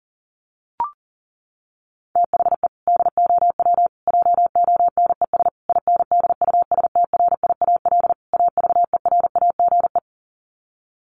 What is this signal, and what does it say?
Morse code, heard as a keyed tone, about 40 wpm, 700 Hz. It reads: THE DOW JONES INDUSTRIAL AVERAGE
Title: Sets of 5 Words 40wpm